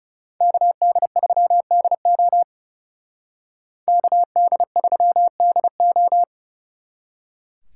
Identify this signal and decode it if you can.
Morse code, heard as a keyed tone, about 36 wpm, 700 Hz. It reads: KD3DO KD3DO
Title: Speed Racing - USA Call Signs 30wpm